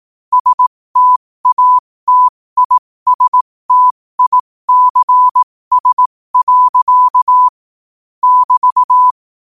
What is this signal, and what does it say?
Morse code, heard as a keyed tone, about 18 wpm, spaced as for 15 wpm, 1000 Hz. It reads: STATISTICS. <BT>
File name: qotdfiles / 15wpm